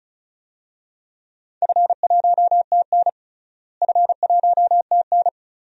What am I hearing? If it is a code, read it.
Morse code, heard as a keyed tone, about 34 wpm, 700 Hz. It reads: F1TN F1TN
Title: DXCC-100 Call Signs - 4 Characters - Repeated 2x - 35wpm